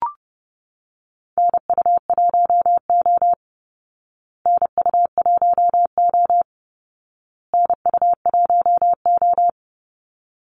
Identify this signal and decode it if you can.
Morse code, heard as a keyed tone, about 30 wpm, 700 Hz. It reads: NU1O NU1O NU1O
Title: Call Signs Repeated 2x - 30wpm